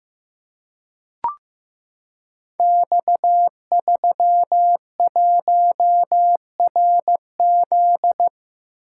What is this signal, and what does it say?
Morse code, heard as a keyed tone, about 15 wpm, 700 Hz. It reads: X31RZ
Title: Code Groups - 5 Letter:Numbers - 15wpm